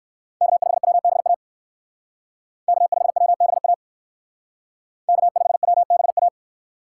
Morse code, heard as a keyed tone, about 55 wpm, 700 Hz. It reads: K5WBA K5WBA K5WBA
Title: Speed Racing - USA 1x3 Call Signs 45wpm